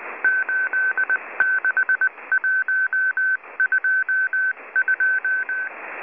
Morse code, heard as a keyed tone, about 19 wpm, 1550 Hz. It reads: 86122